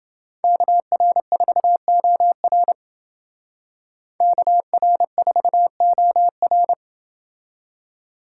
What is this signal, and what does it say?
Morse code, heard as a keyed tone, about 29 wpm, 700 Hz. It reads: KR4OR KR4OR